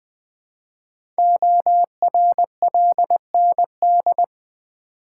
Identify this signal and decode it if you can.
Morse code, heard as a keyed tone, about 20 wpm, 700 Hz. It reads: ORLND
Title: Code Groups - 5 Letters - 20wpm